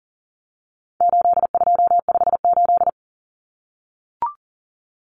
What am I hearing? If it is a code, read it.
Morse code, heard as a keyed tone, about 40 wpm, 700 Hz. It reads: 8258
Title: Common Cut Numbers - 4-Digit 40wpm